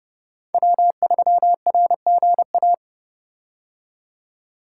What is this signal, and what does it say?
Morse code, heard as a keyed tone, about 30 wpm, 700 Hz. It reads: W3RGA
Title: K1USN SST QSO Practice - 30wpm